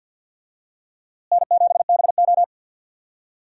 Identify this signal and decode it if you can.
Morse code, heard as a keyed tone, about 51 wpm, 700 Hz. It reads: N7BO
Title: USA Call Signs - 4 Characters - Repeated 2x - 50wpm